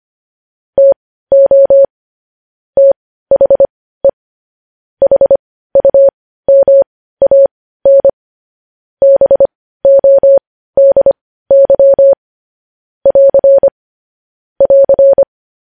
Morse code, heard as a keyed tone, about 25 wpm, spaced as for 15 wpm, 550 Hz. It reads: TO THE HUMAN BODY <AR> <AR>
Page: news headlines in morse code at 15 wpm